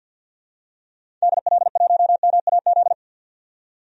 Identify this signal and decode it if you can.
Morse code, heard as a keyed tone, about 53 wpm, 700 Hz. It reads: DL1MAZ
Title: DXCC-100 Call Signs - 6 Characters - 50wpm